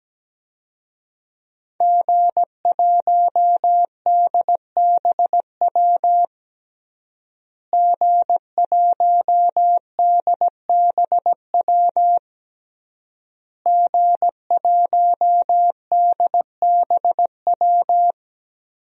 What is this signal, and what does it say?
Morse code, heard as a keyed tone, about 17 wpm, 700 Hz. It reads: G1DBW G1DBW G1DBW
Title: DXCC-100 Call Signs - 5 Characters - Repeated 3x - 17wpm